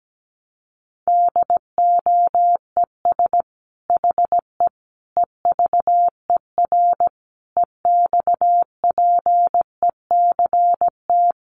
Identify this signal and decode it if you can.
Morse code, heard as a keyed tone, about 17 wpm, 700 Hz. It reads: DOES HE EVER EXPECT
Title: Sentences from Top 1100 Words 17wpm